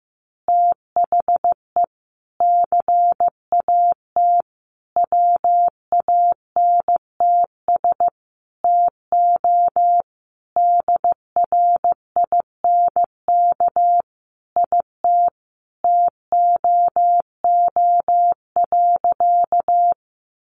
Morse code, heard as a keyed tone, about 15 wpm, 700 Hz. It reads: THE CAT WANTS TO DRINK IT TOO.
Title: Additional Easy Phrases 15wpm